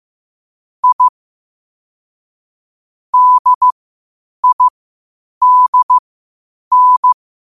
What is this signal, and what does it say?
Morse code, heard as a keyed tone, about 15 wpm, spaced as for 8 wpm, 1000 Hz. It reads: I DIDN
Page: Quotes for Fri, 15 Aug 2025 in Morse Code at 8 words per minute.